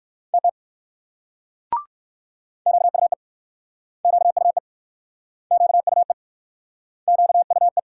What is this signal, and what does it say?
Morse code, heard as a keyed tone, about 61 wpm, 700 Hz. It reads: TT QUE QUE QUE QUE